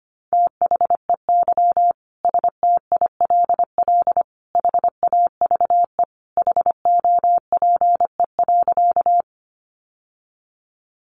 Morse code, heard as a keyed tone, about 25 wpm, 700 Hz. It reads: THEY STILL HAVE HOPE.
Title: Sentences from Top 1000 Words 25wpm